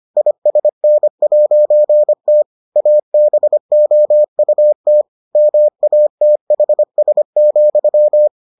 Morse code, heard as a keyed tone, about 25 wpm, 600 Hz. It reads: ISN'T ABOUT MATHS,